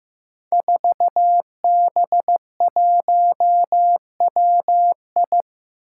Morse code, heard as a keyed tone, about 15 wpm, 700 Hz. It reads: 4B1WI
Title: Five Challenging Letter-Number Combinations 15wpm